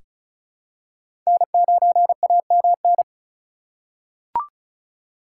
Morse code, heard as a keyed tone, about 34 wpm, 700 Hz. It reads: N9AMN
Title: USA Call Signs - 5 Characters - 2x Character Spacing - 35wpm